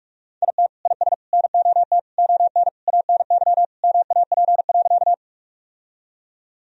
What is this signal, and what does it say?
Morse code, heard as a keyed tone, about 46 wpm, 700 Hz. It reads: IT IS NOT ON ANY MAP.
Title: Sentences from Top 1000 Words - Review Part 2 - 45wpm